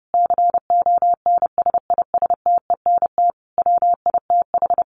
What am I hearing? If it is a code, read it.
Morse code, heard as a keyed tone, about 30 wpm, 700 Hz. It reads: CONSISTENT WITH
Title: Sentences from Top 1500 Words 30wpm